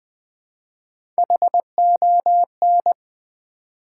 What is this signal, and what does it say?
Morse code, heard as a keyed tone, about 20 wpm, 700 Hz. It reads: HON
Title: Build A Word - 3 Letters At A Time - 6-Letter Words - 20wpm